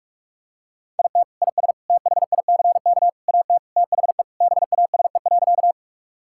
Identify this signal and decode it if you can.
Morse code, heard as a keyed tone, about 42 wpm, 700 Hz. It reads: IT IS THICK AT THE BASE.
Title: Sentences from Top 1600 Words 45wpm